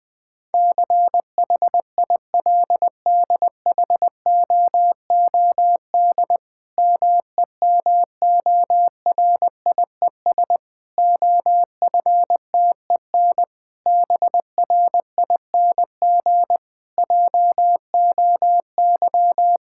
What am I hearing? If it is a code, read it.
Morse code, heard as a keyed tone, about 20 wpm, 700 Hz. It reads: CHILDHOOD MEMORIES OFTEN BRING JOY
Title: Difficult Sentences with Hints - 20wpm